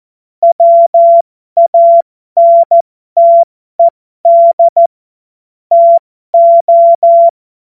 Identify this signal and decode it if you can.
Morse code, heard as a keyed tone, about 14 wpm, spaced as for 12 wpm, 680 Hz. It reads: WANTED TO
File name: Morsekurs / Witze